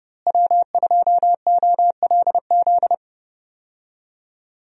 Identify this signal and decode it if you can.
Morse code, heard as a keyed tone, about 30 wpm, 700 Hz. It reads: W2OLZ
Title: USA Call Signs - 5 Characters - Repeated 3x - 30wpm